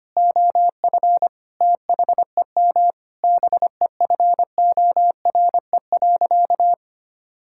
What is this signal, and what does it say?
Morse code, heard as a keyed tone, about 25 wpm, 700 Hz. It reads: OF THEM BEFORE.
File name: practice / 25wpm / Sentences with 9-Word Phrases - 25wpm